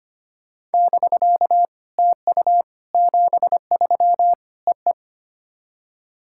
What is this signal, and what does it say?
Morse code, heard as a keyed tone, about 25 wpm, 690 Hz. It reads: <BK> TU 73 E E
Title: SOTA QSO Practice 25wpm